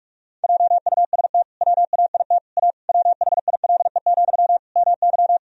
Morse code, heard as a keyed tone, about 46 wpm, 700 Hz. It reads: JUST WAIT A WHILE, MY
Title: Sentences from Top 1000 Words - Review Part 1 - 45wpm